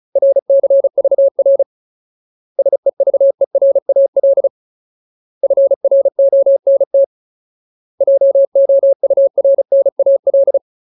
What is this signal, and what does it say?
Morse code, heard as a keyed tone, about 34 wpm, 550 Hz. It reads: RCVR SEVERAL FRONT JOURNAL
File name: Ultimate Word List / 35WPM Ultimate